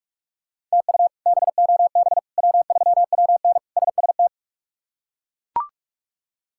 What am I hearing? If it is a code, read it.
Morse code, heard as a keyed tone, about 41 wpm, 700 Hz. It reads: TU BOB W3WN SST
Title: K1USN SST QSO Practice - 45wpm